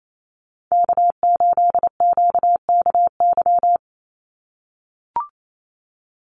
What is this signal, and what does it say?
Morse code, heard as a keyed tone, about 28 wpm, 700 Hz. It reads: K8QKY
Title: Call Signs - 28wpm